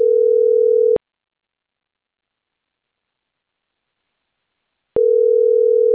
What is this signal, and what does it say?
Ringback tone (North America), 440 + 480 Hz.